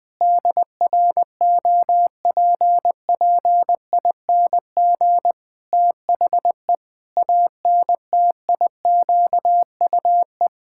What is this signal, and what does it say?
Morse code, heard as a keyed tone, about 20 wpm, 700 Hz. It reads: DROPPING THE ANTIQUE
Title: Pangram Challenge v2 - 20wpm